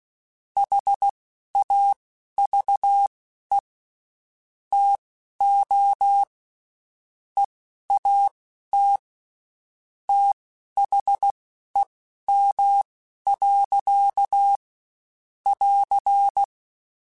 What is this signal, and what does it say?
Morse code, heard as a keyed tone, about 16 wpm, spaced as for 11 wpm, 800 Hz. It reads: HAVE TO EAT THEM. <AR>
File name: Spaced / Chuckles Spaced-mp3s / SoftWords sp @15